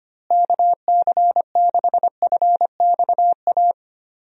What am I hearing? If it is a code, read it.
Morse code, heard as a keyed tone, about 25 wpm, 700 Hz. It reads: KC6FXA
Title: USA Call Signs - 6 Characters - Repeated 2x - 25wpm